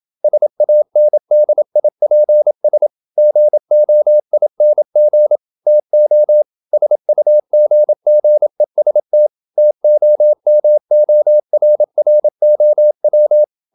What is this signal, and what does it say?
Morse code, heard as a keyed tone, about 27 wpm, 600 Hz. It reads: SANDIPS GOING TO SUGGEST TOMORROW
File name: Never 0027wpm